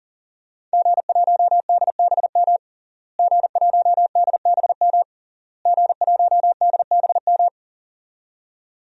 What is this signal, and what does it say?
Morse code, heard as a keyed tone, about 40 wpm, 700 Hz. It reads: G1DBM G1DBM G1DBM
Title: DXCC-100 Call Signs - 5 Characters - Repeated 3x - 40wpm